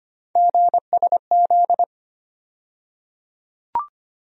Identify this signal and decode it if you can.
Morse code, heard as a keyed tone, about 25 wpm, 700 Hz. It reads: GSZ
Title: USA Call Sign Postfixes - 3 Letters 25wpm